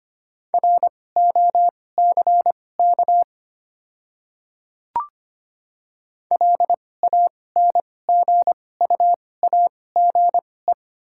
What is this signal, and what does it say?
Morse code, heard as a keyed tone, about 25 wpm, spaced as for 18 wpm, 690 Hz. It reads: ROCK LANGUAGE
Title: 2x Character Spacing - Top 500 Words - 25wpm